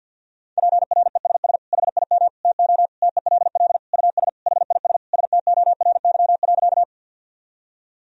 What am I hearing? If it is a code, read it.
Morse code, heard as a keyed tone, about 51 wpm, 700 Hz. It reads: PRESS HIM TO TELL US HIS STORY.
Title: Sentences from Top 1000 Words - Review Part 1 - 50wpm